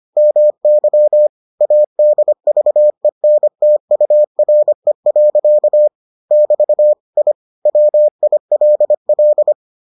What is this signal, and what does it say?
Morse code, heard as a keyed tone, about 25 wpm, 600 Hz. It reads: MY ADVENTURE. <BT> I WILL